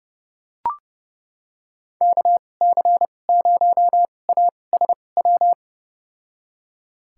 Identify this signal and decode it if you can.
Morse code, heard as a keyed tone, about 31 wpm, spaced as for 22 wpm, 700 Hz. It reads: KC0ASW
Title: USA Call Signs - 2x Character Spacing - 30wpm